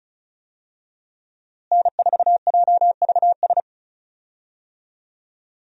Morse code, heard as a keyed tone, about 36 wpm, 700 Hz. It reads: N4JVS
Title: USA Call Signs - 5 Characters - Repeated 3x - 35wpm